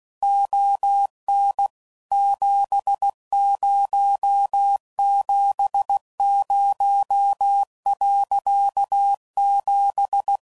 Morse code, heard as a keyed tone, about 16 wpm, 800 Hz. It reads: ON 7070.7
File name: CW Code Groups @15WPM / QSOs / kc7haa